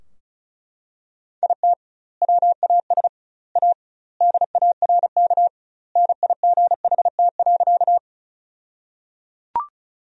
Morse code, heard as a keyed tone, about 36 wpm, 700 Hz. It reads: IT WAS A DARK NIGHT.
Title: 2x Word-Spacing - Sentences from Top 500 Words - 35wpm